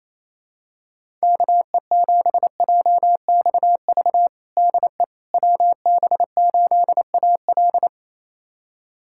Morse code, heard as a keyed tone, about 28 wpm, 700 Hz. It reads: KE7JXV DE WB8AL
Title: USA Callsign DE Callsign 28wpm